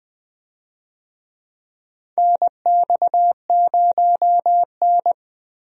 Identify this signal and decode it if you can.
Morse code, heard as a keyed tone, about 20 wpm, 700 Hz. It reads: NX0N
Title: USA Call Signs - 4 Characters - Repeated 2x - 20wpm